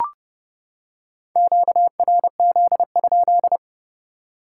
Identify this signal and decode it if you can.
Morse code, heard as a keyed tone, about 30 wpm, 700 Hz. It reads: QRZ?
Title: Top 180 QSO Elements 30wpm